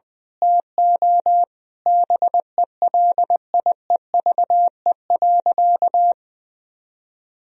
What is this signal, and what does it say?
Morse code, heard as a keyed tone, about 20 wpm, 700 Hz. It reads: TO BELIEVE.
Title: Sentences from Top 1000 Words - Review Part 4 - 20wpm